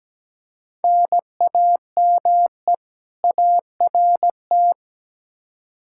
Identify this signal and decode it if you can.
Morse code, heard as a keyed tone, about 17 wpm, 700 Hz. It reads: NAME ART
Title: K1USN SST QSO Practice 17wpm